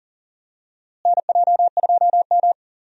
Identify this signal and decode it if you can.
Morse code, heard as a keyed tone, about 40 wpm, 700 Hz. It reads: NJ2M
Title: USA Call Signs - 4 Characters - Repeated 2x - 40wpm